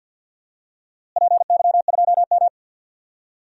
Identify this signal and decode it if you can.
Morse code, heard as a keyed tone, about 48 wpm, 700 Hz. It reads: PY2M
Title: DXCC-100 Call Signs - 4 Characters - Repeated 2x - 50wpm